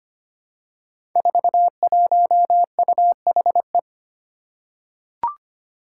Morse code, heard as a keyed tone, about 25 wpm, 700 Hz. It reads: 41UHE
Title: Five Challenging Letter-Number Combinations 25wpm